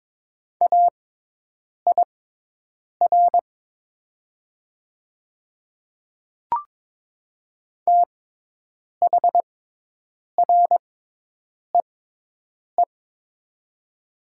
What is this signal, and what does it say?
Morse code, heard as a keyed tone, about 22 wpm, spaced as for 8 wpm, 700 Hz. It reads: AIR THREE
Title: 6x Character Spacing - Top 200 Words - 22wpm